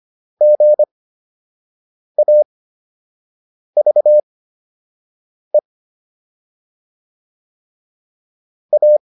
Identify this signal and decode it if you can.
Morse code, heard as a keyed tone, about 25 wpm, spaced as for 6 wpm, 600 Hz. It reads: GAVE A